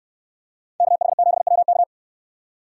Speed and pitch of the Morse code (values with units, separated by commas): 65 wpm, 700 Hz